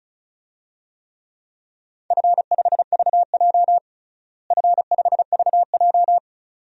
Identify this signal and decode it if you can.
Morse code, heard as a keyed tone, about 36 wpm, 690 Hz. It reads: F5VJ F5VJ
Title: DXCC-100 Call Signs - 4 Characters - Repeated 2x - 35wpm